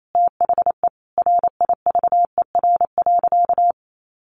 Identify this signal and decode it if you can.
Morse code, heard as a keyed tone, about 28 wpm, 700 Hz. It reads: THE RIVER.
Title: Sentences from Top 1500 Words 28wpm